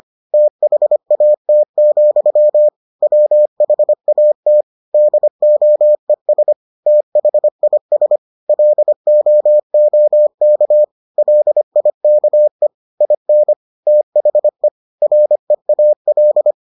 Morse code, heard as a keyed tone, about 25 wpm, 600 Hz. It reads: THAT, WHAT DOES THIS LOOK LIKE IN THE REAL